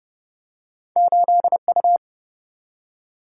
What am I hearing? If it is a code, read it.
Morse code, heard as a keyed tone, about 30 wpm, 700 Hz. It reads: 8U
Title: Single-Letter Single-Number 30wpm